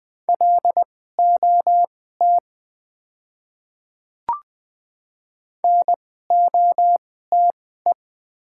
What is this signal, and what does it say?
Morse code, heard as a keyed tone, about 20 wpm, spaced as for 14 wpm, 700 Hz. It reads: LOT NOTE
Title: 2x Character Spacing - Top 500 Words - 20wpm